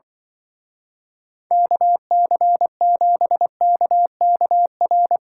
Morse code, heard as a keyed tone, about 23 wpm, 700 Hz. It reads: KC7KKR
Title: Speed Racing - USA 2x3 Call Signs 20wpm